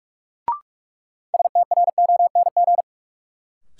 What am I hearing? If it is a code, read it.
Morse code, heard as a keyed tone, about 46 wpm, 700 Hz. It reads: STRONG
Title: Sentences from Top 500 Words - New Word Drill - 45wpm